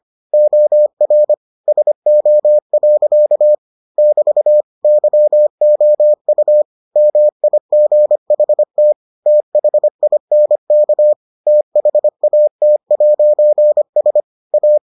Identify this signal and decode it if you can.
Morse code, heard as a keyed tone, about 25 wpm, 600 Hz. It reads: OR SO. <BT> YOU MIGHT THINK THAT'S A